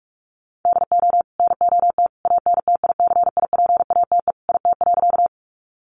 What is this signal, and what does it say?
Morse code, heard as a keyed tone, about 44 wpm, 700 Hz. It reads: DO NOT ANTICIPATE IT.
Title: Sentences from Top 1500 Words 45wpm